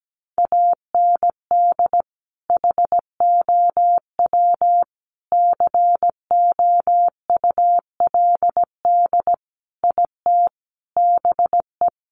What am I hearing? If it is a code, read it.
Morse code, heard as a keyed tone, about 17 wpm, 700 Hz. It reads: AND HOW COULD IT BE
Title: Sentences from Top 1200 Words 17wpm